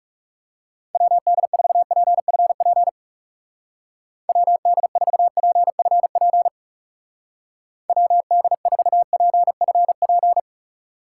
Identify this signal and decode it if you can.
Morse code, heard as a keyed tone, about 38 wpm, 700 Hz. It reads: WD4PFP WD4PFP WD4PFP